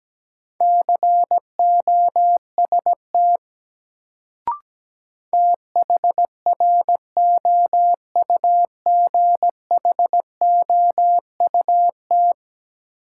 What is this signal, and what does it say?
Morse code, heard as a keyed tone, about 17 wpm, 700 Hz. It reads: COST THROUGHOUT
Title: Sentences from Top 800 Words - New Word Drill - 17wpm